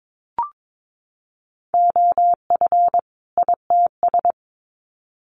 Morse code, heard as a keyed tone, about 22 wpm, 700 Hz. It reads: OF ITS
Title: Sets of 2 Words 22wpm